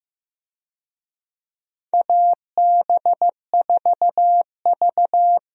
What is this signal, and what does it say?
Morse code, heard as a keyed tone, about 15 wpm, 700 Hz. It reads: AB4V
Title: USA Call Signs - 4 Characters - 15wpm